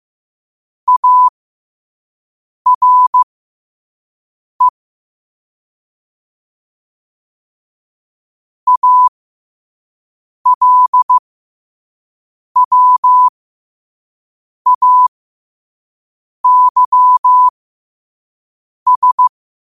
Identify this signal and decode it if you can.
Morse code, heard as a keyed tone, about 15 wpm, spaced as for 5 wpm, 1000 Hz. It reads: ARE ALWAYS
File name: qotdfiles / 5wpm